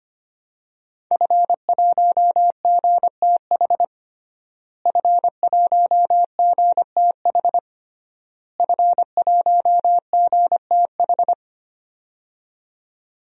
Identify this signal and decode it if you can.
Morse code, heard as a keyed tone, about 25 wpm, 700 Hz. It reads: F1GTH F1GTH F1GTH
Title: DXCC-100 Call Signs - 5 Characters - Repeated 3x - 25wpm